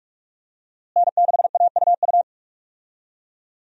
Morse code, heard as a keyed tone, about 45 wpm, 700 Hz. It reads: N6AUU
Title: USA Call Signs - 5 Characters - Repeated 3x - 45wpm